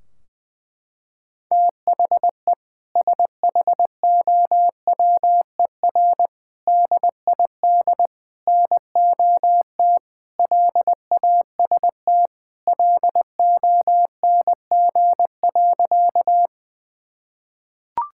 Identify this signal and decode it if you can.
Morse code, heard as a keyed tone, about 20 wpm, 700 Hz. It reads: THE SHOWER DID NOT LAST LONG.
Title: Sentences from Top 1600 Words 20wpm